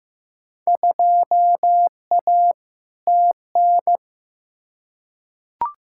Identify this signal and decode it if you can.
Morse code, heard as a keyed tone, about 15 wpm, 700 Hz. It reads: 2A TN
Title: Field Day Exchanges - 15wpm